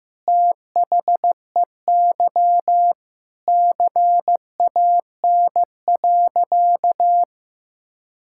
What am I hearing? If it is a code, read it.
Morse code, heard as a keyed tone, about 15 wpm, 700 Hz. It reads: THEY CAN.
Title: Sentences from Top 1000 Words - Review Part 5 - 15wpm